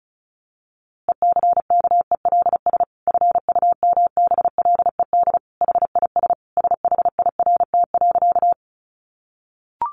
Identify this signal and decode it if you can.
Morse code, heard as a keyed tone, about 34 wpm, 700 Hz. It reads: ECKELS FUMBLED HIS SHIRT.
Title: A Sound Of Thunder 35wpm